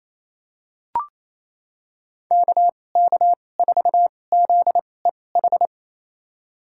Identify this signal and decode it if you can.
Morse code, heard as a keyed tone, about 28 wpm, spaced as for 20 wpm, 700 Hz. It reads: KK4ZEH
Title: USA Call Signs - 6 Characters - 2x Character Spacing - 28wpm